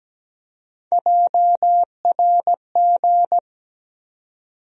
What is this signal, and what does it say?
Morse code, heard as a keyed tone, about 17 wpm, 700 Hz. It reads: JRG
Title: USA Call Sign Postfixes - 3 Letters 17wpm